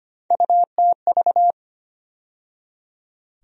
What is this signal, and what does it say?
Morse code, heard as a keyed tone, about 25 wpm, 700 Hz. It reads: UTV
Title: USA Call Sign Postfixes - 3 Letters 25wpm